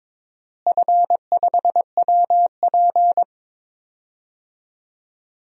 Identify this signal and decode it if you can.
Morse code, heard as a keyed tone, about 22 wpm, 700 Hz. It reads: F5WP
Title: DXCC-100 Call Signs - 4 Characters - Repeated 3x - 22wpm